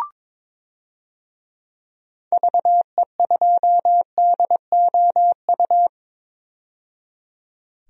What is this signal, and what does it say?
Morse code, heard as a keyed tone, about 22 wpm, 700 Hz. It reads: VE2DOU
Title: DXCC-100 Call Signs - 6 Characters - 22wpm